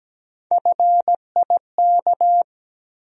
Morse code, heard as a keyed tone, about 17 wpm, 700 Hz. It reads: FIK
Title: USA Call Sign Postfixes - 3 Letters 17wpm